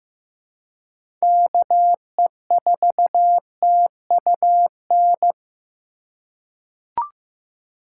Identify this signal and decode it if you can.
Morse code, heard as a keyed tone, about 15 wpm, 700 Hz. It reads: KE4TUN
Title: USA Call Signs - 6 Characters - 3x Character Spacing - 15wpm